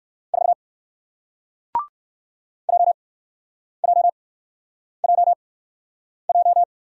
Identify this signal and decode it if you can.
Morse code, heard as a keyed tone, about 60 wpm, spaced as for 9 wpm, 700 Hz. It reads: 4 JJJJ